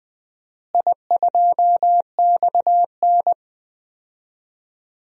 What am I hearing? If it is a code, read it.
Morse code, heard as a keyed tone, about 21 wpm, 700 Hz. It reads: I2XN
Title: DXCC-100 Call Signs - 4 Characters - Repeated 2x - 20wpm